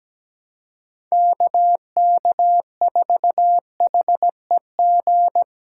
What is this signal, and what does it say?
Morse code, heard as a keyed tone, about 17 wpm, 700 Hz. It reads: KK4HEG
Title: K1USN SST QSO Practice - 17wpm